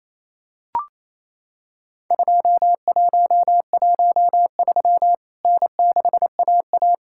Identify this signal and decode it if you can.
Morse code, heard as a keyed tone, about 28 wpm, 700 Hz. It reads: 2113 N6AA
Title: Worked All Europe DX Contest - QTC Exchange - 28wpm